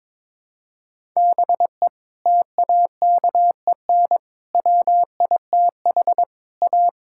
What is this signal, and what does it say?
Morse code, heard as a keyed tone, about 22 wpm, 700 Hz. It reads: BE TAKEN WITH A
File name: practice / 22wpm / Sets of 7 Words - 22wpm